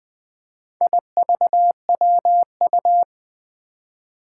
Morse code, heard as a keyed tone, about 20 wpm, 700 Hz. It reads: IVWU
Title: Four Challenging Letter-Number Combinations 20wpm